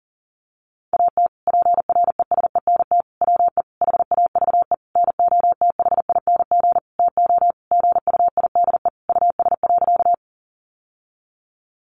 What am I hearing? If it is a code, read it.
Morse code, heard as a keyed tone, about 40 wpm, 700 Hz. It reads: AT PRESENT WE HAVE NOTHING TO GUIDE US.
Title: Sentences from Top 800 Words 40wpm